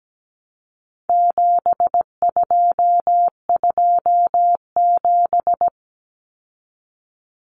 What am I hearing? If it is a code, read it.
Morse code, heard as a keyed tone, about 17 wpm, 700 Hz. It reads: 7227
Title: Common Cut Numbers - 4-Digit 17wpm